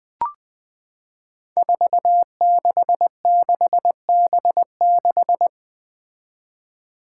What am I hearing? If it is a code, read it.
Morse code, heard as a keyed tone, about 20 wpm, 700 Hz. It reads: 466B6
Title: Five Challenging Letter-Number Combinations 20wpm